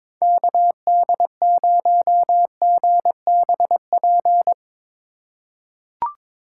Morse code, heard as a keyed tone, about 22 wpm, 700 Hz. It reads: KD0GBP
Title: USA Call Signs - 6 Characters - 4x Character Spacing - 22wpm